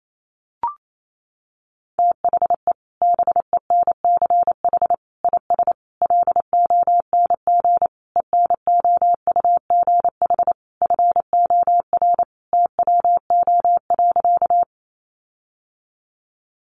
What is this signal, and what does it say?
Morse code, heard as a keyed tone, about 28 wpm, 700 Hz. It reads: THE BENCH IS LONG ENOUGH FOR TWO.
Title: Sentences from Top 1400 Words 28wpm